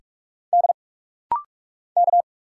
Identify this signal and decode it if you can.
Morse code, heard as a keyed tone, about 46 wpm, 700 Hz. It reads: D K
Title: Single Letters - Rapid-Fire 45wpm